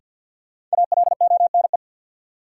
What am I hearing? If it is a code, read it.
Morse code, heard as a keyed tone, about 48 wpm, 700 Hz. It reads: ALONE
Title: Sentences from Top 1000 Words - New Word Drill - 50wpm